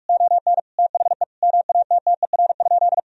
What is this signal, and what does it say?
Morse code, heard as a keyed tone, about 46 wpm, 700 Hz. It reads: ON THE MATTER?
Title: Sentences from Top 1000 Words - Review Part 4 - 45wpm